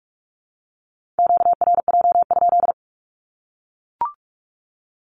Morse code, heard as a keyed tone, about 44 wpm, 700 Hz. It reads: QRJ?
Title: Q-Codes Repeated 3x 45wpm